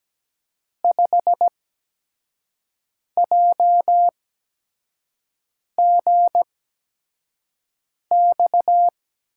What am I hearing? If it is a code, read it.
Morse code, heard as a keyed tone, about 17 wpm, spaced as for 5 wpm, 700 Hz. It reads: 5JGX
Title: USA Call Signs - 6 Characters - 8x Character Spacing - 17wpm